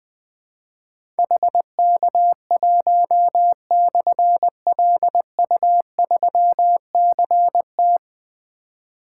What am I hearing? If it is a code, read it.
Morse code, heard as a keyed tone, about 20 wpm, 700 Hz. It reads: HK1/LU3CT
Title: Difficult Call Signs 20wpm